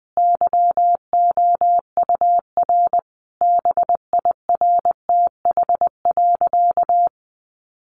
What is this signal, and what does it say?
Morse code, heard as a keyed tone, about 20 wpm, 700 Hz. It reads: YOUR BIRTH.
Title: Sentences from Top 1300 Words 20wpm